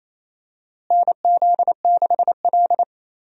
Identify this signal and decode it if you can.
Morse code, heard as a keyed tone, about 28 wpm, 700 Hz. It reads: NZ6L
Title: USA Call Signs - 4 Characters - Repeated 2x - 28wpm